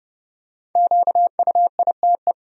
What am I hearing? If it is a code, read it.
Morse code, heard as a keyed tone, about 30 wpm, 700 Hz. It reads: QUITE
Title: Sentences from Top 1000 Words - Review Part 1 - 30wpm